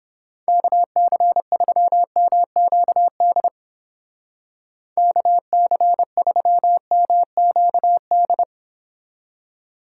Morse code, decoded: KC3MQD KC3MQD

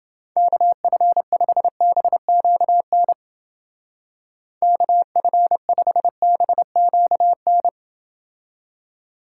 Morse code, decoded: KF5BQN KF5BQN

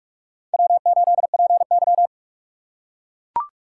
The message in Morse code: W8PY

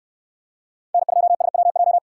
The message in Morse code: N2SWJ

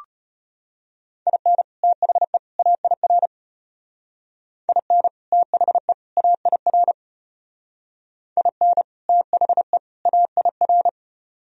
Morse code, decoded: IN THE AIR IN THE AIR IN THE AIR

